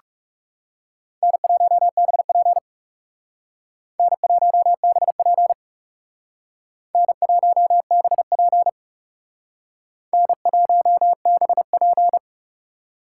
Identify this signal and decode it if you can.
Morse code, heard as a keyed tone, about 37 wpm, 700 Hz. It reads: N1BP N1BP N1BP N1BP
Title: Speed Racing - USA 1x2 Call Signs 30wpm